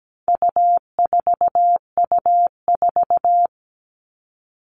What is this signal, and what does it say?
Morse code, heard as a keyed tone, about 17 wpm, 700 Hz. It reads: U4U4
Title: All Cut Numbers - 4-Digit 17wpm